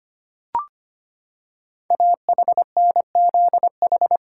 Morse code, heard as a keyed tone, about 25 wpm, 700 Hz. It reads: AHNZH
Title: Code Groups - 5 Letters - 25wpm